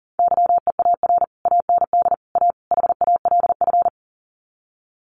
Morse code, heard as a keyed tone, about 40 wpm, 700 Hz. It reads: YEAR AND A HALF
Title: Sets of 4 Words 40wpm